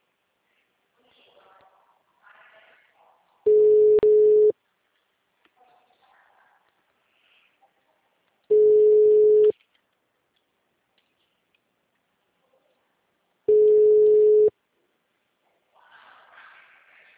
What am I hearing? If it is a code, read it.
Ringback tone (Europe, 425 Hz).
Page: Rio Claro 2 - SP